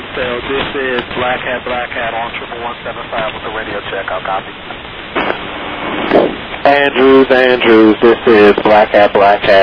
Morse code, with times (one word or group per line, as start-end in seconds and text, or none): none